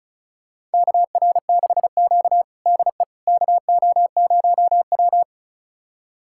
Morse code, heard as a keyed tone, about 36 wpm, 700 Hz. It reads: KR6Q DE KO0W
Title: USA Callsign DE Callsign - 4 Characters - 35wpm